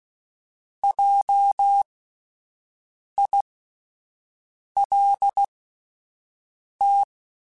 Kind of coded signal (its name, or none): Morse code